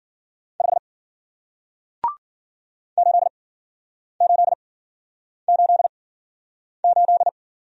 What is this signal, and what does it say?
Morse code, heard as a keyed tone, about 53 wpm, spaced as for 9 wpm, 700 Hz. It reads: 5 8888